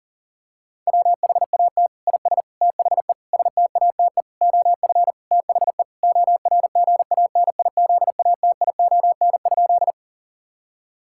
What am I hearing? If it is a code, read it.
Morse code, heard as a keyed tone, about 40 wpm, 700 Hz. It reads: WHAT IS THE STATE OF THE ORGANIZATION?
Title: Sentences from Top 1000 Words - Review Part 5 - 40wpm